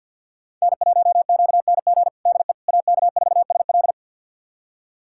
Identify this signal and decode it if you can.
Morse code, heard as a keyed tone, about 52 wpm, 700 Hz. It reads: N1QNG DE AK4SL
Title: USA Callsign DE Callsign - 5 Characters - 50wpm